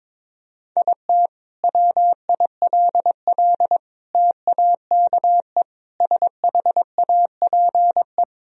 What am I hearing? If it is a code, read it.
Morse code, heard as a keyed tone, about 22 wpm, 700 Hz. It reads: IT WILL TAKE SHAPE
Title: Sentences from Top 1000 Words - Review Part 3 - 22wpm